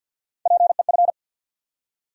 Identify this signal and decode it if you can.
Morse code, heard as a keyed tone, about 50 wpm, 700 Hz. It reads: PEF
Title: USA Call Sign Postfixes - 3 Letters 50wpm